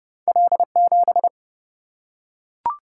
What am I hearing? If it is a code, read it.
Morse code, heard as a keyed tone, about 29 wpm, 700 Hz. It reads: L7
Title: Single-Letter Single-Number 30wpm